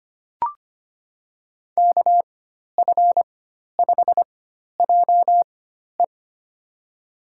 Morse code, heard as a keyed tone, about 25 wpm, spaced as for 12 wpm, 700 Hz. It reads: KF5JE
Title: USA Call Signs - 5 Characters - 4x Character Spacing - 25wpm